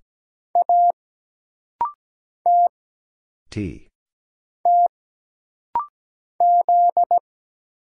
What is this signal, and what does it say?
Morse code, heard as a keyed tone, about 17 wpm, 700 Hz. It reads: A T T Z
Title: Single Letter-Number Alt - Rapid-Fire 17wpm